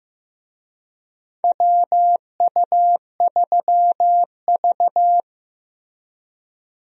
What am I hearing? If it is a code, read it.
Morse code, heard as a keyed tone, about 15 wpm, 700 Hz. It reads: WU3V
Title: USA Call Signs - 4 Characters - Repeated 3x - 15wpm